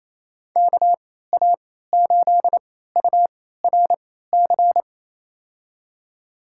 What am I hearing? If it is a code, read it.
Morse code, heard as a keyed tone, about 28 wpm, spaced as for 16 wpm, 700 Hz. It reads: KA8URC
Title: USA Call Signs - 6 Characters - 3x Character Spacing - 28wpm